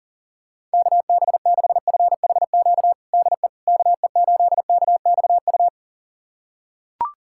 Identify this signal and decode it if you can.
Morse code, heard as a keyed tone, about 40 wpm, 700 Hz. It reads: KB6FHQ DE KE8KXU
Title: USA Callsign DE Callsign 40wpm